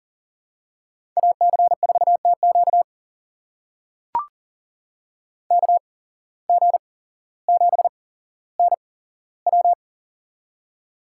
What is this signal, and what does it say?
Morse code, heard as a keyed tone, about 40 wpm, spaced as for 17 wpm, 700 Hz. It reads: AC4TQ KG7NW
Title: USA Call Signs - 5 Characters - 8x Character Spacing - 40wpm